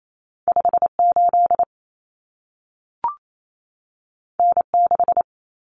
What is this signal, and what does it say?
Morse code, heard as a keyed tone, about 28 wpm, 700 Hz. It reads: 58 N6
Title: Common Cut Numbers - 2-Digit 28wpm